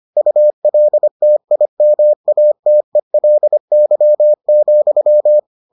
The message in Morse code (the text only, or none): ULTIMATELY,